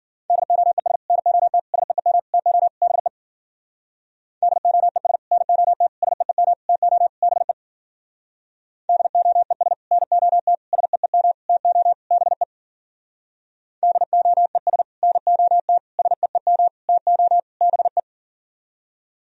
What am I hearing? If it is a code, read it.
Morse code, heard as a keyed tone, about 51 wpm, 700 Hz. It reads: DOES NOT SEEM TO BE DOES NOT SEEM TO BE DOES NOT SEEM TO BE DOES NOT SEEM TO BE